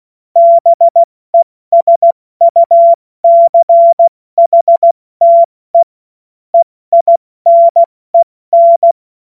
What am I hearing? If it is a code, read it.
Morse code, heard as a keyed tone, about 16 wpm, spaced as for 14 wpm, 680 Hz. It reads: BESUCHTE EINEN
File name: Morsekurs / Witze / Witz 07. März 2023 70 F80